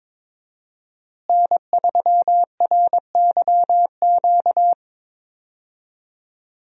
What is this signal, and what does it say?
Morse code, heard as a keyed tone, about 21 wpm, 700 Hz. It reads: N3RYQ
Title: USA Call Signs - 5 Characters - Repeated 2x - 22wpm